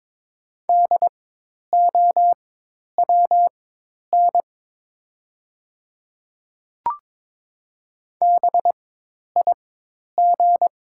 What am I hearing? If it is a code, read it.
Morse code, heard as a keyed tone, about 22 wpm, spaced as for 10 wpm, 700 Hz. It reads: DOWN BIG
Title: 4x Character Spacing - Top 200 Words - 22wpm